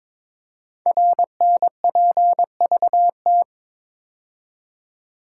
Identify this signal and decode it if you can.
Morse code, heard as a keyed tone, about 23 wpm, 700 Hz. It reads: RNPVT
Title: Code Groups - 5 Letters - 22wpm